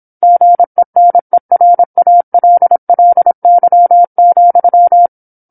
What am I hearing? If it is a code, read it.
Morse code, heard as a keyed tone, about 27 wpm, 700 Hz. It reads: GENERALLY,
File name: 08-Chapter-08 26 WPM